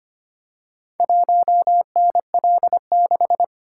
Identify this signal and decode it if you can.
Morse code, heard as a keyed tone, about 25 wpm, 690 Hz. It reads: 1NL6
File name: practice / 25wpm / Four Letter-Number Combinations - 25wpm